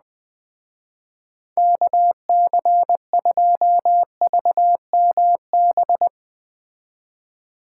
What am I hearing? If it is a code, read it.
Morse code, heard as a keyed tone, about 20 wpm, 700 Hz. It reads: KC2VMB
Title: Speed Racing - USA 2x3 Call Signs 17wpm